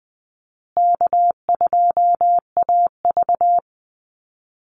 Morse code, heard as a keyed tone, about 20 wpm, 700 Hz. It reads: K2AV
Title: Easy Call Signs Repeated 3x - 20wpm